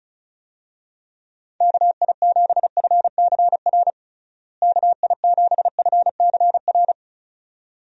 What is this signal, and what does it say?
Morse code, heard as a keyed tone, about 36 wpm, 700 Hz. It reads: KI7FCR KI7FCR
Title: USA Call Signs - 6 Characters - Repeated 2x - 35wpm